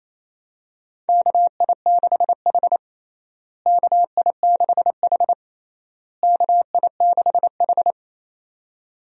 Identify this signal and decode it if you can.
Morse code, heard as a keyed tone, about 28 wpm, 700 Hz. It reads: KI6H KI6H KI6H
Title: USA Call Signs - 4 Characters - Repeated 3x - 28wpm